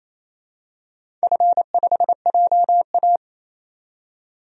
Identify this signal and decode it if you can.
Morse code, heard as a keyed tone, about 27 wpm, 700 Hz. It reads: F5JA
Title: DXCC-100 Call Signs - 4 Characters - 28wpm